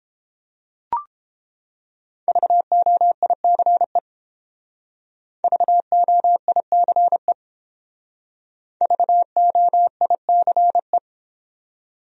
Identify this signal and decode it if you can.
Morse code, heard as a keyed tone, about 29 wpm, 700 Hz. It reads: VOICE VOICE VOICE